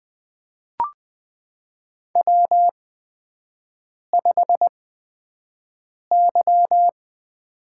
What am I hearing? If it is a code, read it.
Morse code, heard as a keyed tone, about 20 wpm, spaced as for 5 wpm, 700 Hz. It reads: W5Y
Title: USA Call Signs - 8x Character Spacing - 20wpm